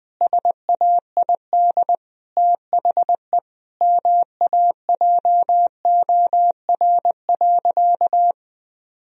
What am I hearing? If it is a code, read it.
Morse code, heard as a keyed tone, about 21 wpm, 700 Hz. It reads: SAID THE MAJOR.
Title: Sentences from Top 1000 Words - Review Part 4 - 20wpm